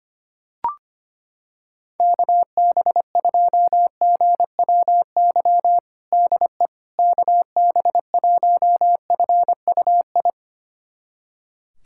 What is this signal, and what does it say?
Morse code, heard as a keyed tone, about 25 wpm, 700 Hz. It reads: KB2GWY DE KB1FUI
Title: USA Callsign DE Callsign - 6 Characters - 25wpm